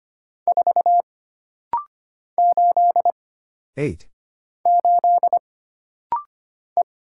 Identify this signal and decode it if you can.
Morse code, heard as a keyed tone, about 25 wpm, 700 Hz. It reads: 4 8 8 E
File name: Single Character (Letter:Number:Punctuation:Prosign) - Rapid-Fire - 25wpm